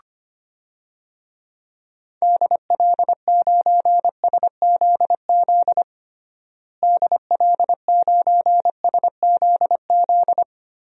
Morse code, heard as a keyed tone, about 25 wpm, 700 Hz. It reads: DL9SZZ DL9SZZ
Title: DXCC-100 Call Signs - 6 Characters - Repeated 2x - 25wpm